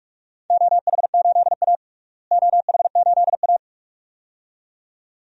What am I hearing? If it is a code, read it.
Morse code, heard as a keyed tone, about 47 wpm, 700 Hz. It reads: OH8A OH8A
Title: DXCC-100 Call Signs - 4 Characters - Repeated 2x - 45wpm